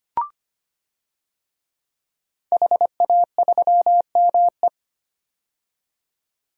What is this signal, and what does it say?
Morse code, heard as a keyed tone, about 25 wpm, 700 Hz. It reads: HA3ME
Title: DXCC-100 Call Signs - 5 Characters - 25wpm